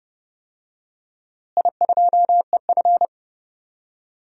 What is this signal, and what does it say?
Morse code, heard as a keyed tone, about 30 wpm, 700 Hz. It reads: I2EF
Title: DXCC-100 Call Signs - 4 Characters - Repeated 3x - 30wpm